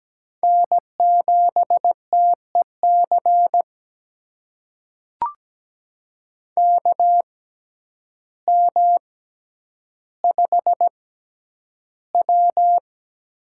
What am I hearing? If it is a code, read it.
Morse code, heard as a keyed tone, about 17 wpm, spaced as for 9 wpm, 700 Hz. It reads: N7TEC KM5W
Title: USA Call Signs - 5 Characters - 6x Character Spacing - 17wpm